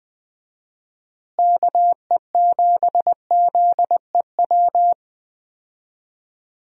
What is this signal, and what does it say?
Morse code, heard as a keyed tone, about 20 wpm, 700 Hz. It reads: KE7ZEW
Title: USA Call Signs - 6 Characters - Repeated 3x - 20wpm